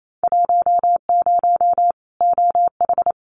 Morse code, heard as a keyed tone, about 28 wpm, 700 Hz. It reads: 10 OH
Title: Winter Field Day Exchanges 28wpm